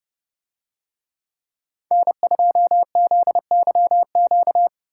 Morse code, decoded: N2ZYQ

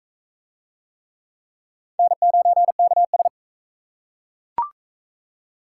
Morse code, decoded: N9KS

